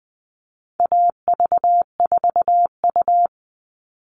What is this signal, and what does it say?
Morse code, heard as a keyed tone, about 20 wpm, 700 Hz. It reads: AV4U
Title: All Cut Numbers - 4-Digit 20wpm